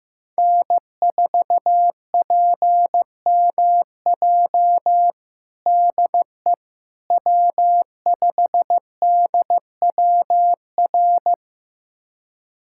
Morse code, heard as a keyed tone, about 15 wpm, 700 Hz. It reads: N4PMJ DE W5DWR
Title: USA Callsign DE Callsign - 5 Characters - 15wpm